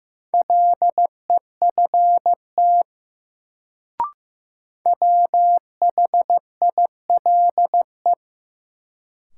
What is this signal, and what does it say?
Morse code, heard as a keyed tone, about 15 wpm, 700 Hz. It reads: LEFT WHILE
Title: Sentences from Top 300 Words - New Word Drill - 15wpm